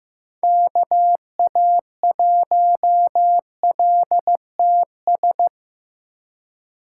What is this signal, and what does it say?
Morse code, heard as a keyed tone, about 16 wpm, 700 Hz. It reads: KA1LTS
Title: USA Call Signs - 6 Characters - 4x Character Spacing - 15wpm